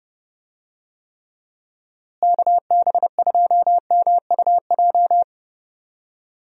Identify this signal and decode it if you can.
Morse code, heard as a keyed tone, about 30 wpm, 700 Hz. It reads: KB2MUJ
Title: USA Call Signs - 6 Characters - Repeated 3x - 30wpm